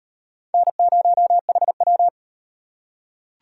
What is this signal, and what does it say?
Morse code, heard as a keyed tone, about 37 wpm, 700 Hz. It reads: N0HW